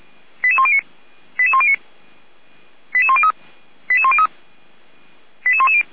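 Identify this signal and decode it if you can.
Five-tone selective call (ZVEI), heard as quick runs of five tones, tones of about 80 ms. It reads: 88109 88109 88103 88103 88110
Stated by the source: ZVEI five-tone call 88102 received on SDR, logged by Feuerwehr Orlamünde.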